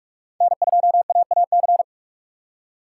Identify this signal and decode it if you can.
Morse code, heard as a keyed tone, about 47 wpm, 700 Hz. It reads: N2AAC
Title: USA Call Signs - 5 Characters - 3x Character Spacing - 45wpm